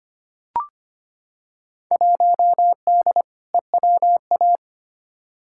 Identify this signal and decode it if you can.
Morse code, heard as a keyed tone, about 25 wpm, 700 Hz. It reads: 1D EWA
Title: Field Day Exchanges - 25wpm